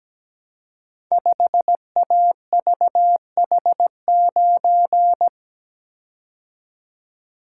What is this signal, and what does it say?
Morse code, heard as a keyed tone, about 17 wpm, 700 Hz. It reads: 5AVH9
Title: Code Groups - 5 Letter:Numbers - 17wpm